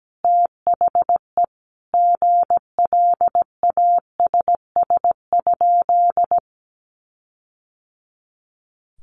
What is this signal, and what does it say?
Morse code, heard as a keyed tone, about 17 wpm, 700 Hz. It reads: THE GLASS?
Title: Sentences from Top 1300 Words 17wpm